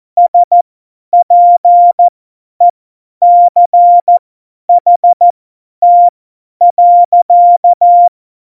Morse code, heard as a keyed tone, about 14 wpm, spaced as for 10 wpm, 700 Hz. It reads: SPECHT.
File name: Morsekurs / Witze / Witz 06. November 2018, 50 F70